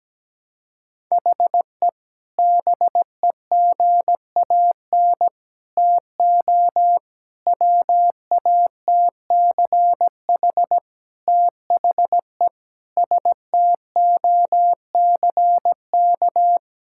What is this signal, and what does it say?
Morse code, heard as a keyed tone, about 17 wpm, 700 Hz. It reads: HE BEGAN TO WATCH THE STOCK
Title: Sentences from Top 1000 Words - Review Part 6 - 17wpm